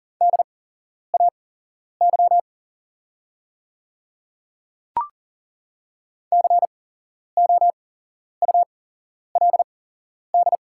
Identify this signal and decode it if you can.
Morse code, heard as a keyed tone, about 40 wpm, spaced as for 11 wpm, 700 Hz. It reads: DAY COULD
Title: 8x Character Spacing - Top 100 Words - 40wpm